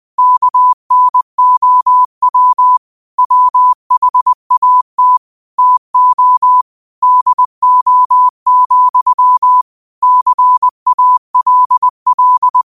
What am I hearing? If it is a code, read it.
Morse code, heard as a keyed tone, about 20 wpm, 1000 Hz. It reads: KNOW WHAT TO DO, CALL